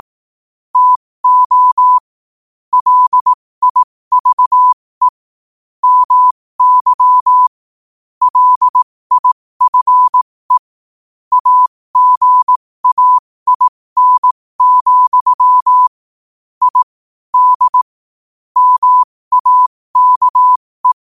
Morse code, heard as a keyed tone, about 18 wpm, spaced as for 15 wpm, 1000 Hz. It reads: TO LIVE MY LIFE AGAIN, I D MAKE
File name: qotdfiles / 15wpm